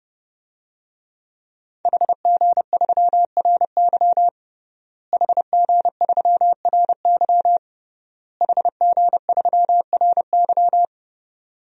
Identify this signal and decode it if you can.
Morse code, heard as a keyed tone, about 30 wpm, 700 Hz. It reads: HG3RY HG3RY HG3RY
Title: DXCC-100 Call Signs - 5 Characters - Repeated 3x - 30wpm